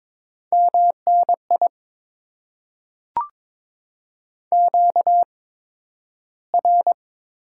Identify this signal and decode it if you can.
Morse code, heard as a keyed tone, about 22 wpm, spaced as for 12 wpm, 700 Hz. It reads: MNI QR
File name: Top 90 QSO Elements - 8x Character Spacing - 22wpm